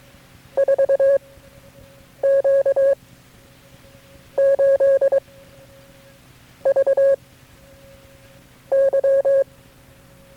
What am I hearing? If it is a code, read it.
Morse code, heard as a keyed tone, about 23 wpm, spaced as for 7 wpm, 550 Hz. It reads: 4Q 8 V Y
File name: KB6MT-tape3-side1